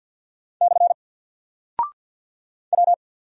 Morse code, heard as a keyed tone, about 48 wpm, 700 Hz. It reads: / W
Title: Single Character (Letter:Number:Punctuation:Prosign) Alt 50wpm